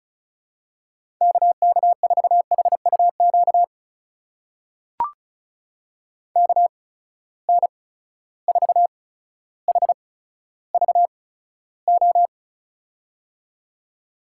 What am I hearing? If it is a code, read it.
Morse code, heard as a keyed tone, about 35 wpm, spaced as for 15 wpm, 700 Hz. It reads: KK4HUQ KN4HVO
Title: USA Call Signs - 6 Characters - 8x Character Spacing - 35wpm